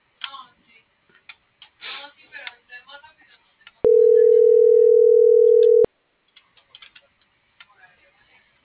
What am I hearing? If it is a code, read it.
Ringback tone (North America), 440 + 480 Hz.